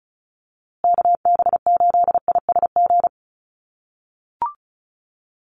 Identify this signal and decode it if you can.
Morse code, heard as a keyed tone, about 34 wpm, 700 Hz. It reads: KB8ISG
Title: USA Call Signs Repeated 2x 35wpm